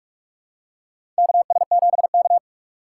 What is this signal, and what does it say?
Morse code, heard as a keyed tone, about 46 wpm, 700 Hz. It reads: KS7K
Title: USA Call Signs - 4 Characters - 2x Character Spacing - 45wpm